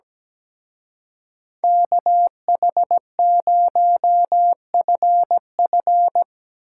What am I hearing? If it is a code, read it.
Morse code, heard as a keyed tone, about 17 wpm, 700 Hz. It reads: KH0FF
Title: Speed Racing - USA 2x2 Call Signs 17wpm